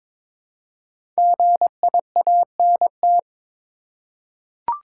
Morse code, heard as a keyed tone, about 23 wpm, 700 Hz. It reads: GIANT
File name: 5 Letter Words - First 3 Letters Given - 22wpm